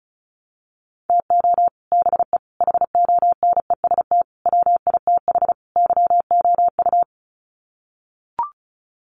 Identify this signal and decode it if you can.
Morse code, heard as a keyed tone, about 34 wpm, 700 Hz. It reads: TO BE HONEST WITH YOU
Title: Sets of 5 Words Encore 35wpm